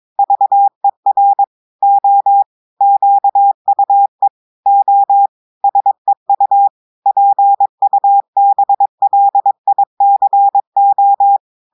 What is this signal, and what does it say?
Morse code, heard as a keyed tone, about 22 wpm, 800 Hz. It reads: VER O QUE O SEU PUBLICO
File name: audios / portugais / portugais texte en clair 2 vitesse 22 mots minutes